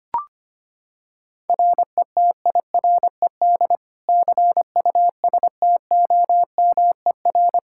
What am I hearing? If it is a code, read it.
Morse code, heard as a keyed tone, about 25 wpm, 700 Hz. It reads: RETIRED CUSTOMER
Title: Practice QSO - Exchange 3 - More Info - 25wpm